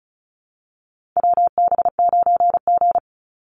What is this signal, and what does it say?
Morse code, heard as a keyed tone, about 34 wpm, 700 Hz. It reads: WB9G
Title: Call Signs - 35wpm